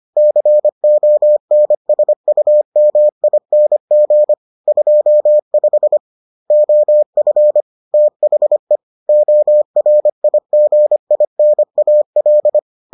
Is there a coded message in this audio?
Yes: Morse code from 0.2 to 12.6 s, heard as a keyed tone, about 25 wpm, 600 Hz. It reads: CONSUMING 25 OF THE ORIGINAL